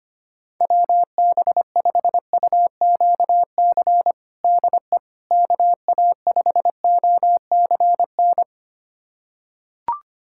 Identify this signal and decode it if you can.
Morse code, heard as a keyed tone, about 25 wpm, 700 Hz. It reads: WB5UQC DE KA5OCN
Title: USA Callsign DE Callsign - 6 Characters - 25wpm